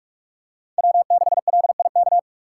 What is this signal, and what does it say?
Morse code, heard as a keyed tone, about 44 wpm, 700 Hz. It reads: W6LIK
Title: USA Call Signs - 5 Characters - 6x Character Spacing - 45wpm